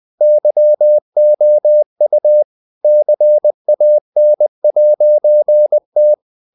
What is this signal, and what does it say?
Morse code, heard as a keyed tone, about 20 wpm, 600 Hz. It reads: YOU CAN'T
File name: CW / Practice / EnglishSayings20